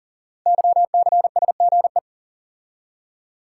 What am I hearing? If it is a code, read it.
Morse code, heard as a keyed tone, about 38 wpm, 690 Hz. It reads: YCSGE
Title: Code Groups - 5 Letters - 40wpm